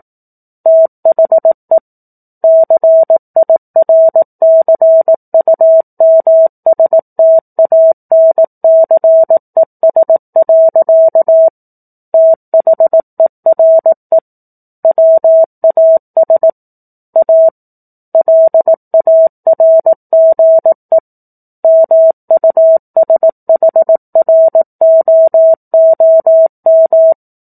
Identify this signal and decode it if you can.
Morse code, heard as a keyed tone, about 18 wpm, 650 Hz. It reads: THE CIRCUMSTANCES. THERE WAS A LARGE MUSHROOM
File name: 04-Chapter-04 18 WPM